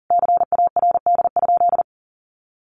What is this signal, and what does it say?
Morse code, heard as a keyed tone, about 37 wpm, 700 Hz. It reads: CARD?
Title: Sentences from Top 1400 Words 40wpm